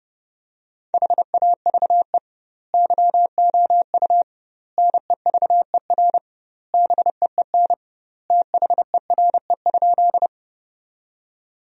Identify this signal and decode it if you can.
Morse code, heard as a keyed tone, about 30 wpm, 700 Hz. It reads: HAVE YOU NEVER BEEN THERE?
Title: 2x Word-Spacing - Sentences from Top 300 Words - 30wpm